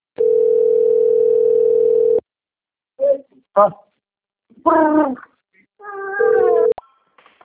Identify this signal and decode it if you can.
Ringback tone (North America), 440 + 480 Hz, with speech.
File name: rg-1600-unknown- - - .14389